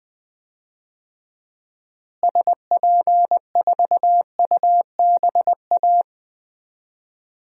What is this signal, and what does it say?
Morse code, heard as a keyed tone, about 20 wpm, 700 Hz. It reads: SP4UBA
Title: DXCC-100 Call Signs - 6 Characters - Repeated 2x - 20wpm